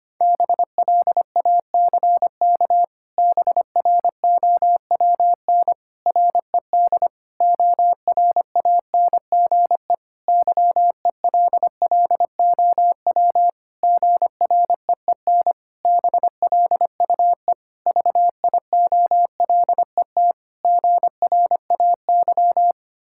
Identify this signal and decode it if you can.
Morse code, heard as a keyed tone, about 25 wpm, 700 Hz. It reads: BLACK BROWN RED ORANGE YELLOW GREEN BLUE VIOLET GRAY
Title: Pangram Challenge v2 - 25wpm